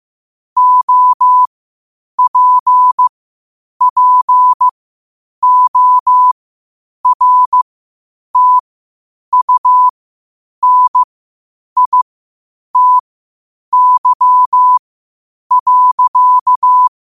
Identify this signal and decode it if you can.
Morse code, heard as a keyed tone, about 15 wpm, spaced as for 8 wpm, 1000 Hz. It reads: OPPORTUNITY.
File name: qotdfiles / 8wpm